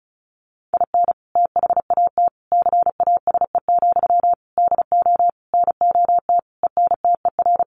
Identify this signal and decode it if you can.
Morse code, heard as a keyed tone, about 34 wpm, 700 Hz. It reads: IN THAT CASE, DO NOT ENTER
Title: Sentences from Top 1300 Words 35wpm